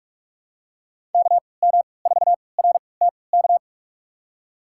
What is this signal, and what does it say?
Morse code, heard as a keyed tone, about 46 wpm, spaced as for 26 wpm, 700 Hz. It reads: KM4RTK
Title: USA Call Signs - 6 Characters - 3x Character Spacing - 45wpm